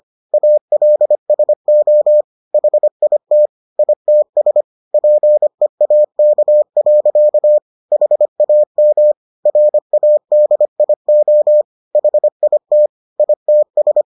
Morse code, heard as a keyed tone, about 25 wpm, 600 Hz. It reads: ALSO HIT ITS PEAK. HAM RADIO HIT ITS